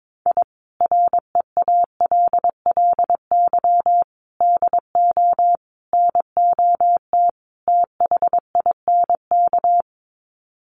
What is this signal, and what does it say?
Morse code, heard as a keyed tone, about 22 wpm, 700 Hz. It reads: I REALLY DO NOT THINK
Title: Sets of 5 Words Encore2 22wpm